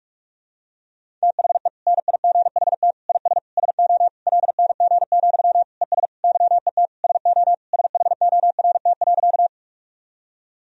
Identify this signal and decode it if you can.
Morse code, heard as a keyed tone, about 46 wpm, 700 Hz. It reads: THE NIGHT IS SO LNG, ES YET SO SHORT.
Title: Sentences from Top 500 Words and Abbreviations 45wpm